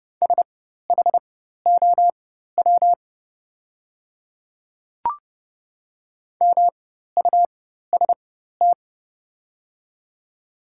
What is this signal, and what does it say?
Morse code, heard as a keyed tone, about 31 wpm, spaced as for 14 wpm, 700 Hz. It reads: SHOW MUST
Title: 4x Character Spacing - Top 300 Words - 30wpm